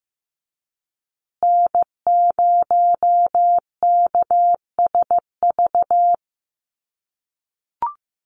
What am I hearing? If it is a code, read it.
Morse code, heard as a keyed tone, about 15 wpm, 700 Hz. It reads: N0KSV
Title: USA Call Signs Repeated 2x 15wpm